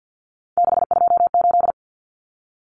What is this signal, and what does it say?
Morse code, heard as a keyed tone, about 48 wpm, 700 Hz. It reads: 628
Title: Common Cut Numbers - 3-Digit 50wpm